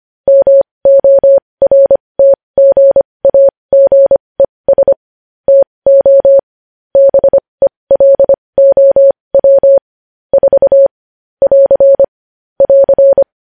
Morse code, heard as a keyed tone, about 25 wpm, spaced as for 20 wpm, 550 Hz. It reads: MORTGAGES TO BELOW 4 <AR> <AR>
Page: news headlines in morse code at 20 wpm